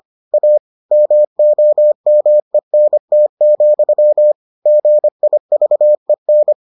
Morse code, heard as a keyed tone, about 25 wpm, 600 Hz. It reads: A MOMENT, GIVEN